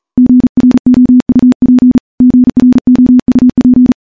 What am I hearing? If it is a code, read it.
Morse code, heard as a keyed tone, about 40 wpm, 250 Hz. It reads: GROUP GROUP
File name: morse / morse-training-40WPM